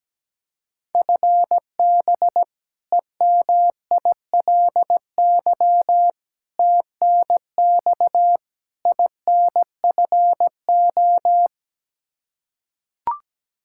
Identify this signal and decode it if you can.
Morse code, heard as a keyed tone, about 17 wpm, 700 Hz. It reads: FB EMILY TNX INFO
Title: Practice QSO - Exchange 3 - More Info - 17wpm